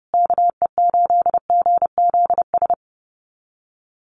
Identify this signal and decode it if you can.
Morse code, heard as a keyed tone, about 30 wpm, 700 Hz. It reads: KE8GZS
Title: USA Call Signs Repeated 3x 30wpm